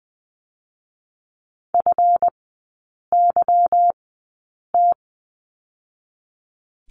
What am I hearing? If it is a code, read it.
Morse code, heard as a keyed tone, about 20 wpm, spaced as for 8 wpm, 700 Hz. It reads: FYT
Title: 218 Lesson 35e - Number 0 - Three Characters 20wpm